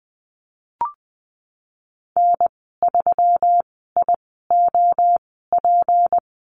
Morse code, heard as a keyed tone, about 19 wpm, spaced as for 14 wpm, 700 Hz. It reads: N3IOP
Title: 193 Lesson 31g - Letter v - Call Signs 20wpm